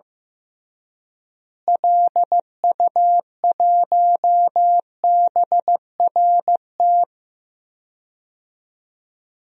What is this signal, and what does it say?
Morse code, heard as a keyed tone, about 15 wpm, 700 Hz. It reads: LU1BRT
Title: DXCC-100 Call Signs - 6 Characters - Repeated 3x - 15wpm